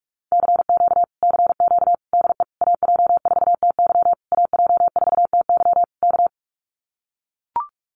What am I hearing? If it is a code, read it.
Morse code, heard as a keyed tone, about 44 wpm, 700 Hz. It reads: CQ CQ DE AJ4TY AJ4TY K
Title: Common QSO Phrases Encore 45wpm